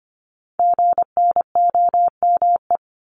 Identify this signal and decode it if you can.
Morse code, heard as a keyed tone, about 25 wpm, 700 Hz. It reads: GNOME
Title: Multi-letter Phonemes 25wpm